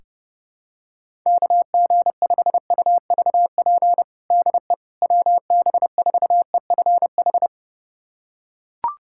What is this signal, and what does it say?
Morse code, heard as a keyed tone, about 30 wpm, 700 Hz. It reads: KG5UVP DE WB4EFH
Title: USA Callsign DE Callsign 30wpm